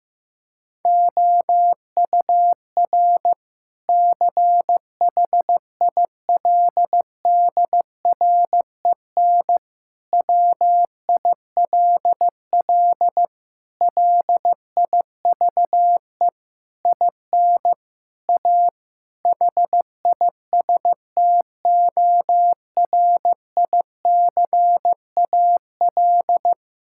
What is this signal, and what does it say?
Morse code, heard as a keyed tone, about 15 wpm, 700 Hz. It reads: OUR CHILDREN WILL LIVE IN A HISTORICAL
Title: Sentences from Top 1000 Words - Review Part 5 - 15wpm